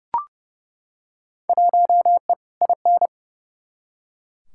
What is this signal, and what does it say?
Morse code, heard as a keyed tone, about 30 wpm, 700 Hz. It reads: 1E IN
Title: Field Day Exchanges Encore - 30wpm